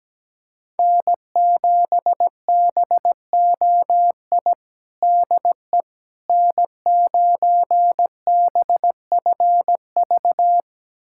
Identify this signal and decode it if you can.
Morse code, heard as a keyed tone, about 17 wpm, 700 Hz. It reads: N7BOI DE N9BFV
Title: USA Callsign DE Callsign - 5 Characters - 17wpm